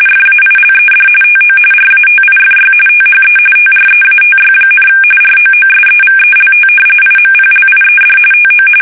RTTY, about 45.45 baud. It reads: FAILED TO RATIFY THE TREATY, ROOSEVELT IMPLIED TO
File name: fsk850 / usnavy3